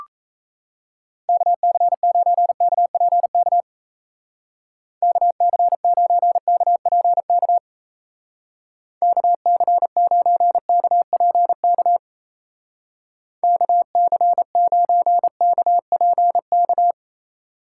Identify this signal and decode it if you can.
Morse code, heard as a keyed tone, about 36 wpm, 700 Hz. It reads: KC9KPK KC9KPK KC9KPK KC9KPK